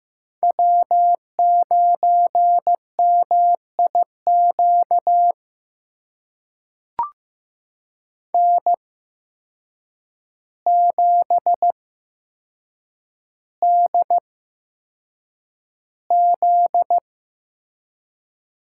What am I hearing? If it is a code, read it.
Morse code, heard as a keyed tone, about 15 wpm, spaced as for 7 wpm, 700 Hz. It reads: W9MIQ N7DZ
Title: USA Call Signs - 5 Characters - 8x Character Spacing - 15wpm